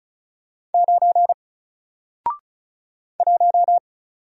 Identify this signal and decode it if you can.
Morse code, heard as a keyed tone, about 36 wpm, 700 Hz. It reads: 9 1
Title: Single Character (Letter:Number:Punctuation:Prosign) Alt 35wpm